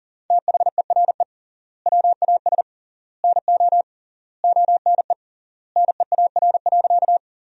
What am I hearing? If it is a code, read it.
Morse code, heard as a keyed tone, about 40 wpm, 700 Hz. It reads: THERE WAS NO ONE NEAR.
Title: 3x Word-Spacing - Sentences from Top 200 Words - 40wpm